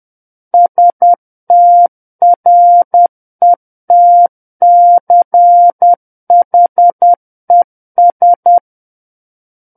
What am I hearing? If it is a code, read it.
Morse code, heard as a keyed tone, about 10 wpm, 700 Hz. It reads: STRETCHES
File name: Book One / 01-Chapter 1 - 10 WPM